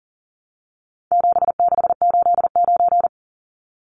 Morse code, heard as a keyed tone, about 40 wpm, 700 Hz. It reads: 7689
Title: Numbers - 4-Digit 40wpm